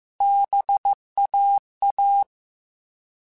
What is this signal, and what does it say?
Morse code, heard as a keyed tone, about 15 wpm, 800 Hz. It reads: BAA
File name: Alphabetical / Alphabetical mp3s / Bs @13